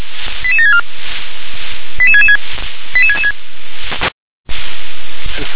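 Five-tone selective call (ZVEI), heard as quick runs of five tones, tones of about 70 ms. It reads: 88764 88666 88666
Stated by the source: Five-tone ZVEI selective call received by SDR.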